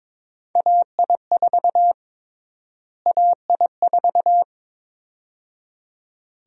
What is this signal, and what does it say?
Morse code, heard as a keyed tone, about 21 wpm, 700 Hz. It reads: AI4 AI4
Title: USA Call Sign Prefixes - Repeated 2x - 22wpm